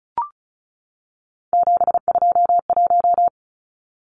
Morse code, heard as a keyed tone, about 34 wpm, 700 Hz. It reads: 721
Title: Numbers - 3-Digit 35wpm